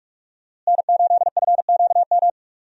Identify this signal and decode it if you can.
Morse code, heard as a keyed tone, about 45 wpm, 700 Hz. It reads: N8FQM
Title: USA Call Signs - 5 Characters - Repeated 2x - 45wpm